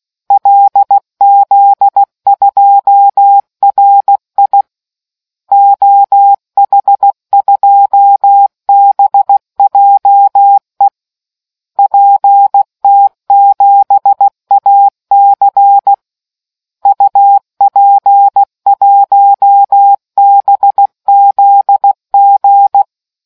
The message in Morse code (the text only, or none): LZ2RI OH2BJE PT7AC UP1BZG